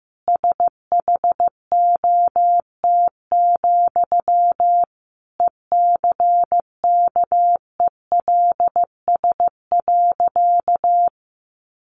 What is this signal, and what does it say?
Morse code, heard as a keyed tone, about 15 wpm, 700 Hz. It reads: SHOT, ECKELS.
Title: A Sound Of Thunder 15wpm